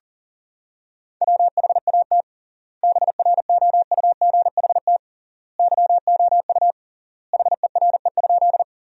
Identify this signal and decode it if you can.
Morse code, heard as a keyed tone, about 40 wpm, 700 Hz. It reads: WHAT BROUGHT YOU HERE?
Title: 3x Word-Spacing - Sentences from Top 500 Words - 40wpm